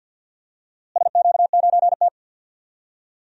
Morse code, heard as a keyed tone, about 50 wpm, 700 Hz. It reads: SQ9T
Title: DXCC-100 Call Signs - 4 Characters - Repeated 2x - 50wpm